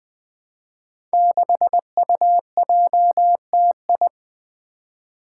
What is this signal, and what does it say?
Morse code, heard as a keyed tone, about 20 wpm, 700 Hz. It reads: 6UJTI
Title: Five Challenging Letter-Number Combinations 20wpm